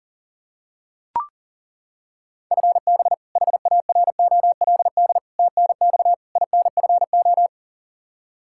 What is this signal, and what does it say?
Morse code, heard as a keyed tone, about 40 wpm, 700 Hz. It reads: FB HAROLD TNX INFO
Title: Practice QSO - Exchange 3 - More Info - 40wpm